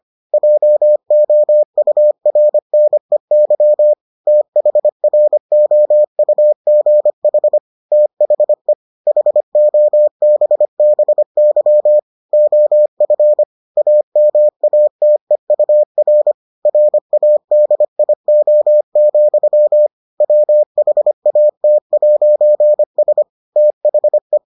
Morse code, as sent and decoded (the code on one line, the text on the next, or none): .--- --- ..- .-. -. -.-- / - .... .-. --- ..- --. .... / - .... / .... --- -... -... -.-- / --- ..-. / .- -- .- - . ..- .-. / .-. .- -.. .. --- --..-- / .-- .... .- - .----. ... / - .... .
JOURNEY THROUGH THE HOBBY OF AMATEUR RADIO, WHAT'S THE